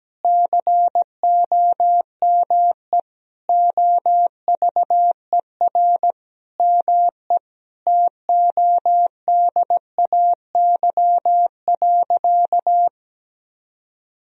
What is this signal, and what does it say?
Morse code, heard as a keyed tone, about 17 wpm, 700 Hz. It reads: COME OVER ME TODAY.
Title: Sentences from Top 1000 Words - Review Part 4 - 17wpm